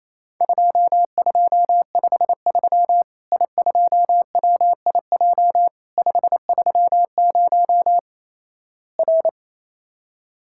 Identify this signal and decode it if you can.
Morse code, heard as a keyed tone, about 28 wpm, 700 Hz. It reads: 2253 I2WIJ 530 R
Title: Worked All Europe DX Contest - QTC Exchange - 28wpm